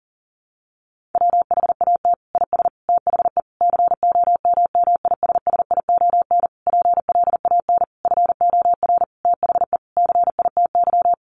Morse code, heard as a keyed tone, about 40 wpm, 700 Hz. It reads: WHAT IS THE COMMISSION PLAN FOR THE CITY
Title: Sentences from Top 1400 Words 40wpm